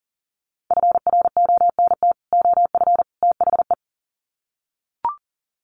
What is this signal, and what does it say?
Morse code, heard as a keyed tone, about 37 wpm, 700 Hz. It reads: FRONT OF THE
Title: Sets of 3 Words 40wpm